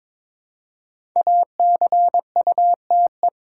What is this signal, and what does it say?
Morse code, heard as a keyed tone, about 22 wpm, 700 Hz. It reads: ACUTE
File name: 5 Letter Words - First 3 Letters Given - 22wpm